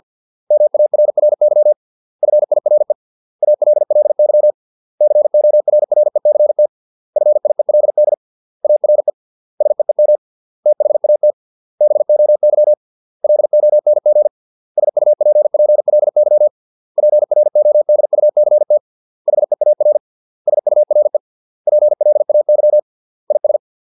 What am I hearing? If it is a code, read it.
Morse code, heard as a keyed tone, about 50 wpm, 600 Hz. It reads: MARRY FIRE ALLY CORRECT FIELD ARE SEEM THAT BOY LONG SUPPLY PRODUCT HEAR SURE PLAY IS